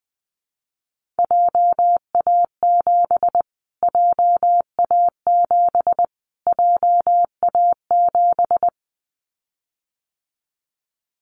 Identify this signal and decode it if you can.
Morse code, heard as a keyed tone, about 20 wpm, 700 Hz. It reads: JA7 JA7 JA7
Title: Callsign Prefixes for Top 25 DXCC - Repeated 3x - 20wpm